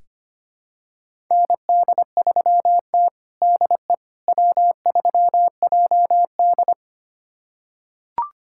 Morse code, heard as a keyed tone, about 25 wpm, 700 Hz. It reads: ND3T DE W3JD
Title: USA Callsign DE Callsign - 4 Characters - 25wpm